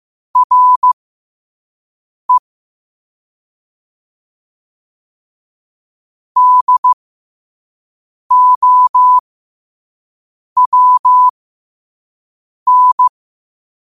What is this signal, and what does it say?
Morse code, heard as a keyed tone, about 15 wpm, spaced as for 5 wpm, 1000 Hz. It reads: RE DOWN
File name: qotdfiles / 5wpm